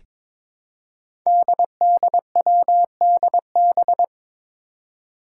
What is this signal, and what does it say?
Morse code, heard as a keyed tone, about 22 wpm, 700 Hz. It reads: DDWDB
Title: Five Challenging Letter-Number Combinations 22wpm